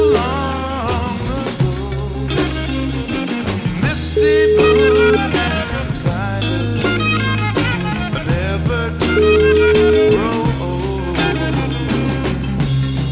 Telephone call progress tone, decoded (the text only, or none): ringback tone